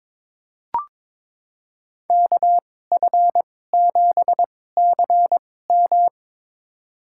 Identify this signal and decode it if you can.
Morse code, heard as a keyed tone, about 22 wpm, spaced as for 16 wpm, 700 Hz. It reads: KF7CM
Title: USA Call Signs - 5 Characters - 2x Character Spacing - 22wpm